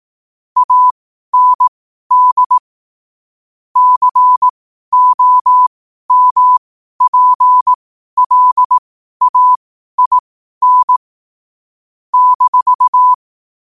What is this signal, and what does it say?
Morse code, heard as a keyed tone, about 18 wpm, spaced as for 12 wpm, 1000 Hz. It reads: AND COMPLAIN -